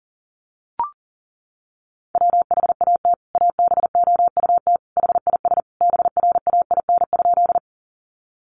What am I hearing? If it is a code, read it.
Morse code, heard as a keyed tone, about 40 wpm, 700 Hz. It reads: WHAT ABOUT HIS BRAIN?
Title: Sentences from Top 1100 Words 40wpm